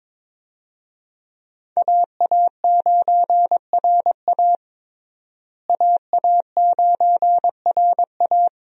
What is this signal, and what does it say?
Morse code, heard as a keyed tone, about 22 wpm, 700 Hz. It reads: AA9RA AA9RA
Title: USA Call Signs - 5 Characters - Repeated 2x - 22wpm